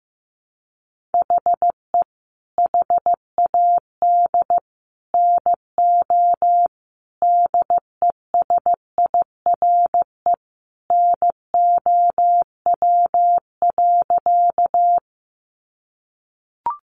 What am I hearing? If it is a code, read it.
Morse code, heard as a keyed tone, about 15 wpm, 700 Hz. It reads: HE HAD NO DESIRE NOW.
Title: Sentences from Top 1100 Words 15wpm